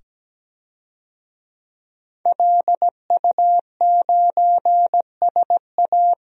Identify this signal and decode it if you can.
Morse code, heard as a keyed tone, about 17 wpm, 700 Hz. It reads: LU9SA
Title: DXCC-100 Call Signs - 5 Characters - 17wpm